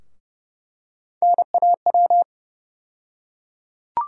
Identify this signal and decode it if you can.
Morse code, heard as a keyed tone, about 32 wpm, 700 Hz. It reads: NAW
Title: USA Call Sign Postfixes - 3 Letters 30wpm